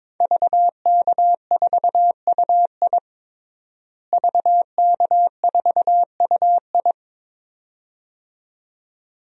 Morse code, heard as a keyed tone, about 22 wpm, 700 Hz. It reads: VK4UI VK4UI
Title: DXCC-100 Call Signs - 5 Characters - Repeated 2x - 22wpm